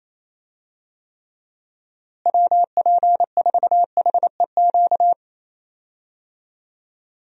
Morse code, heard as a keyed tone, about 28 wpm, 700 Hz. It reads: WP4HEQ
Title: USA Call Signs - 6 Characters - Repeated 2x - 28wpm